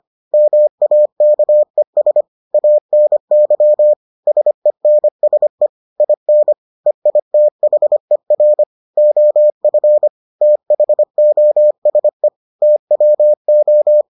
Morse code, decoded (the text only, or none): MAKES ANY SENSE IN EITHER OF THOSE TWO